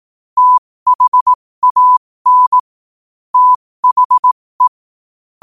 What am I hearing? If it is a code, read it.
Morse code, heard as a keyed tone, about 18 wpm, spaced as for 15 wpm, 1000 Hz. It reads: THAN THE
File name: qotdfiles / 15wpm